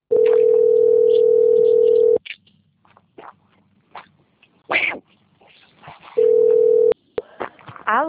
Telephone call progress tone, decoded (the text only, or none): ringback tone